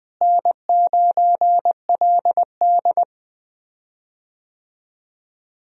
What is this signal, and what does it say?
Morse code, heard as a keyed tone, about 20 wpm, 700 Hz. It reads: N9LD
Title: USA Call Signs - 4 Characters - Repeated 3x - 20wpm